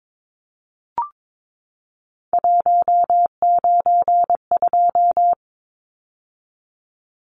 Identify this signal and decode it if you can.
Morse code, heard as a keyed tone, about 22 wpm, 700 Hz. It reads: 192
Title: Numbers - 3-Digit 22wpm